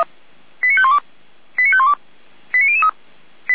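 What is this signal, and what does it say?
Five-tone selective call (ZVEI), heard as quick runs of five tones, tones of about 80 ms. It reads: 89512 89512 89003
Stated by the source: ZVEI five-tone call 89001 received on an SDR.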